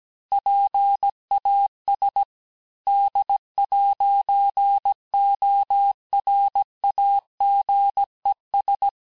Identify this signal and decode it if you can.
Morse code, heard as a keyed tone, about 17 wpm, 800 Hz. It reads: PAS D'ORAGES
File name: cours-du-lundi files / cours-1587-FIN-de-QSO-vitesse-16MPM